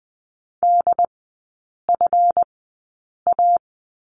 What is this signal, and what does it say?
Morse code, heard as a keyed tone, about 20 wpm, spaced as for 8 wpm, 700 Hz. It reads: DFA
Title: 159 Lesson 26e - Letter g - Three Characters 20wpm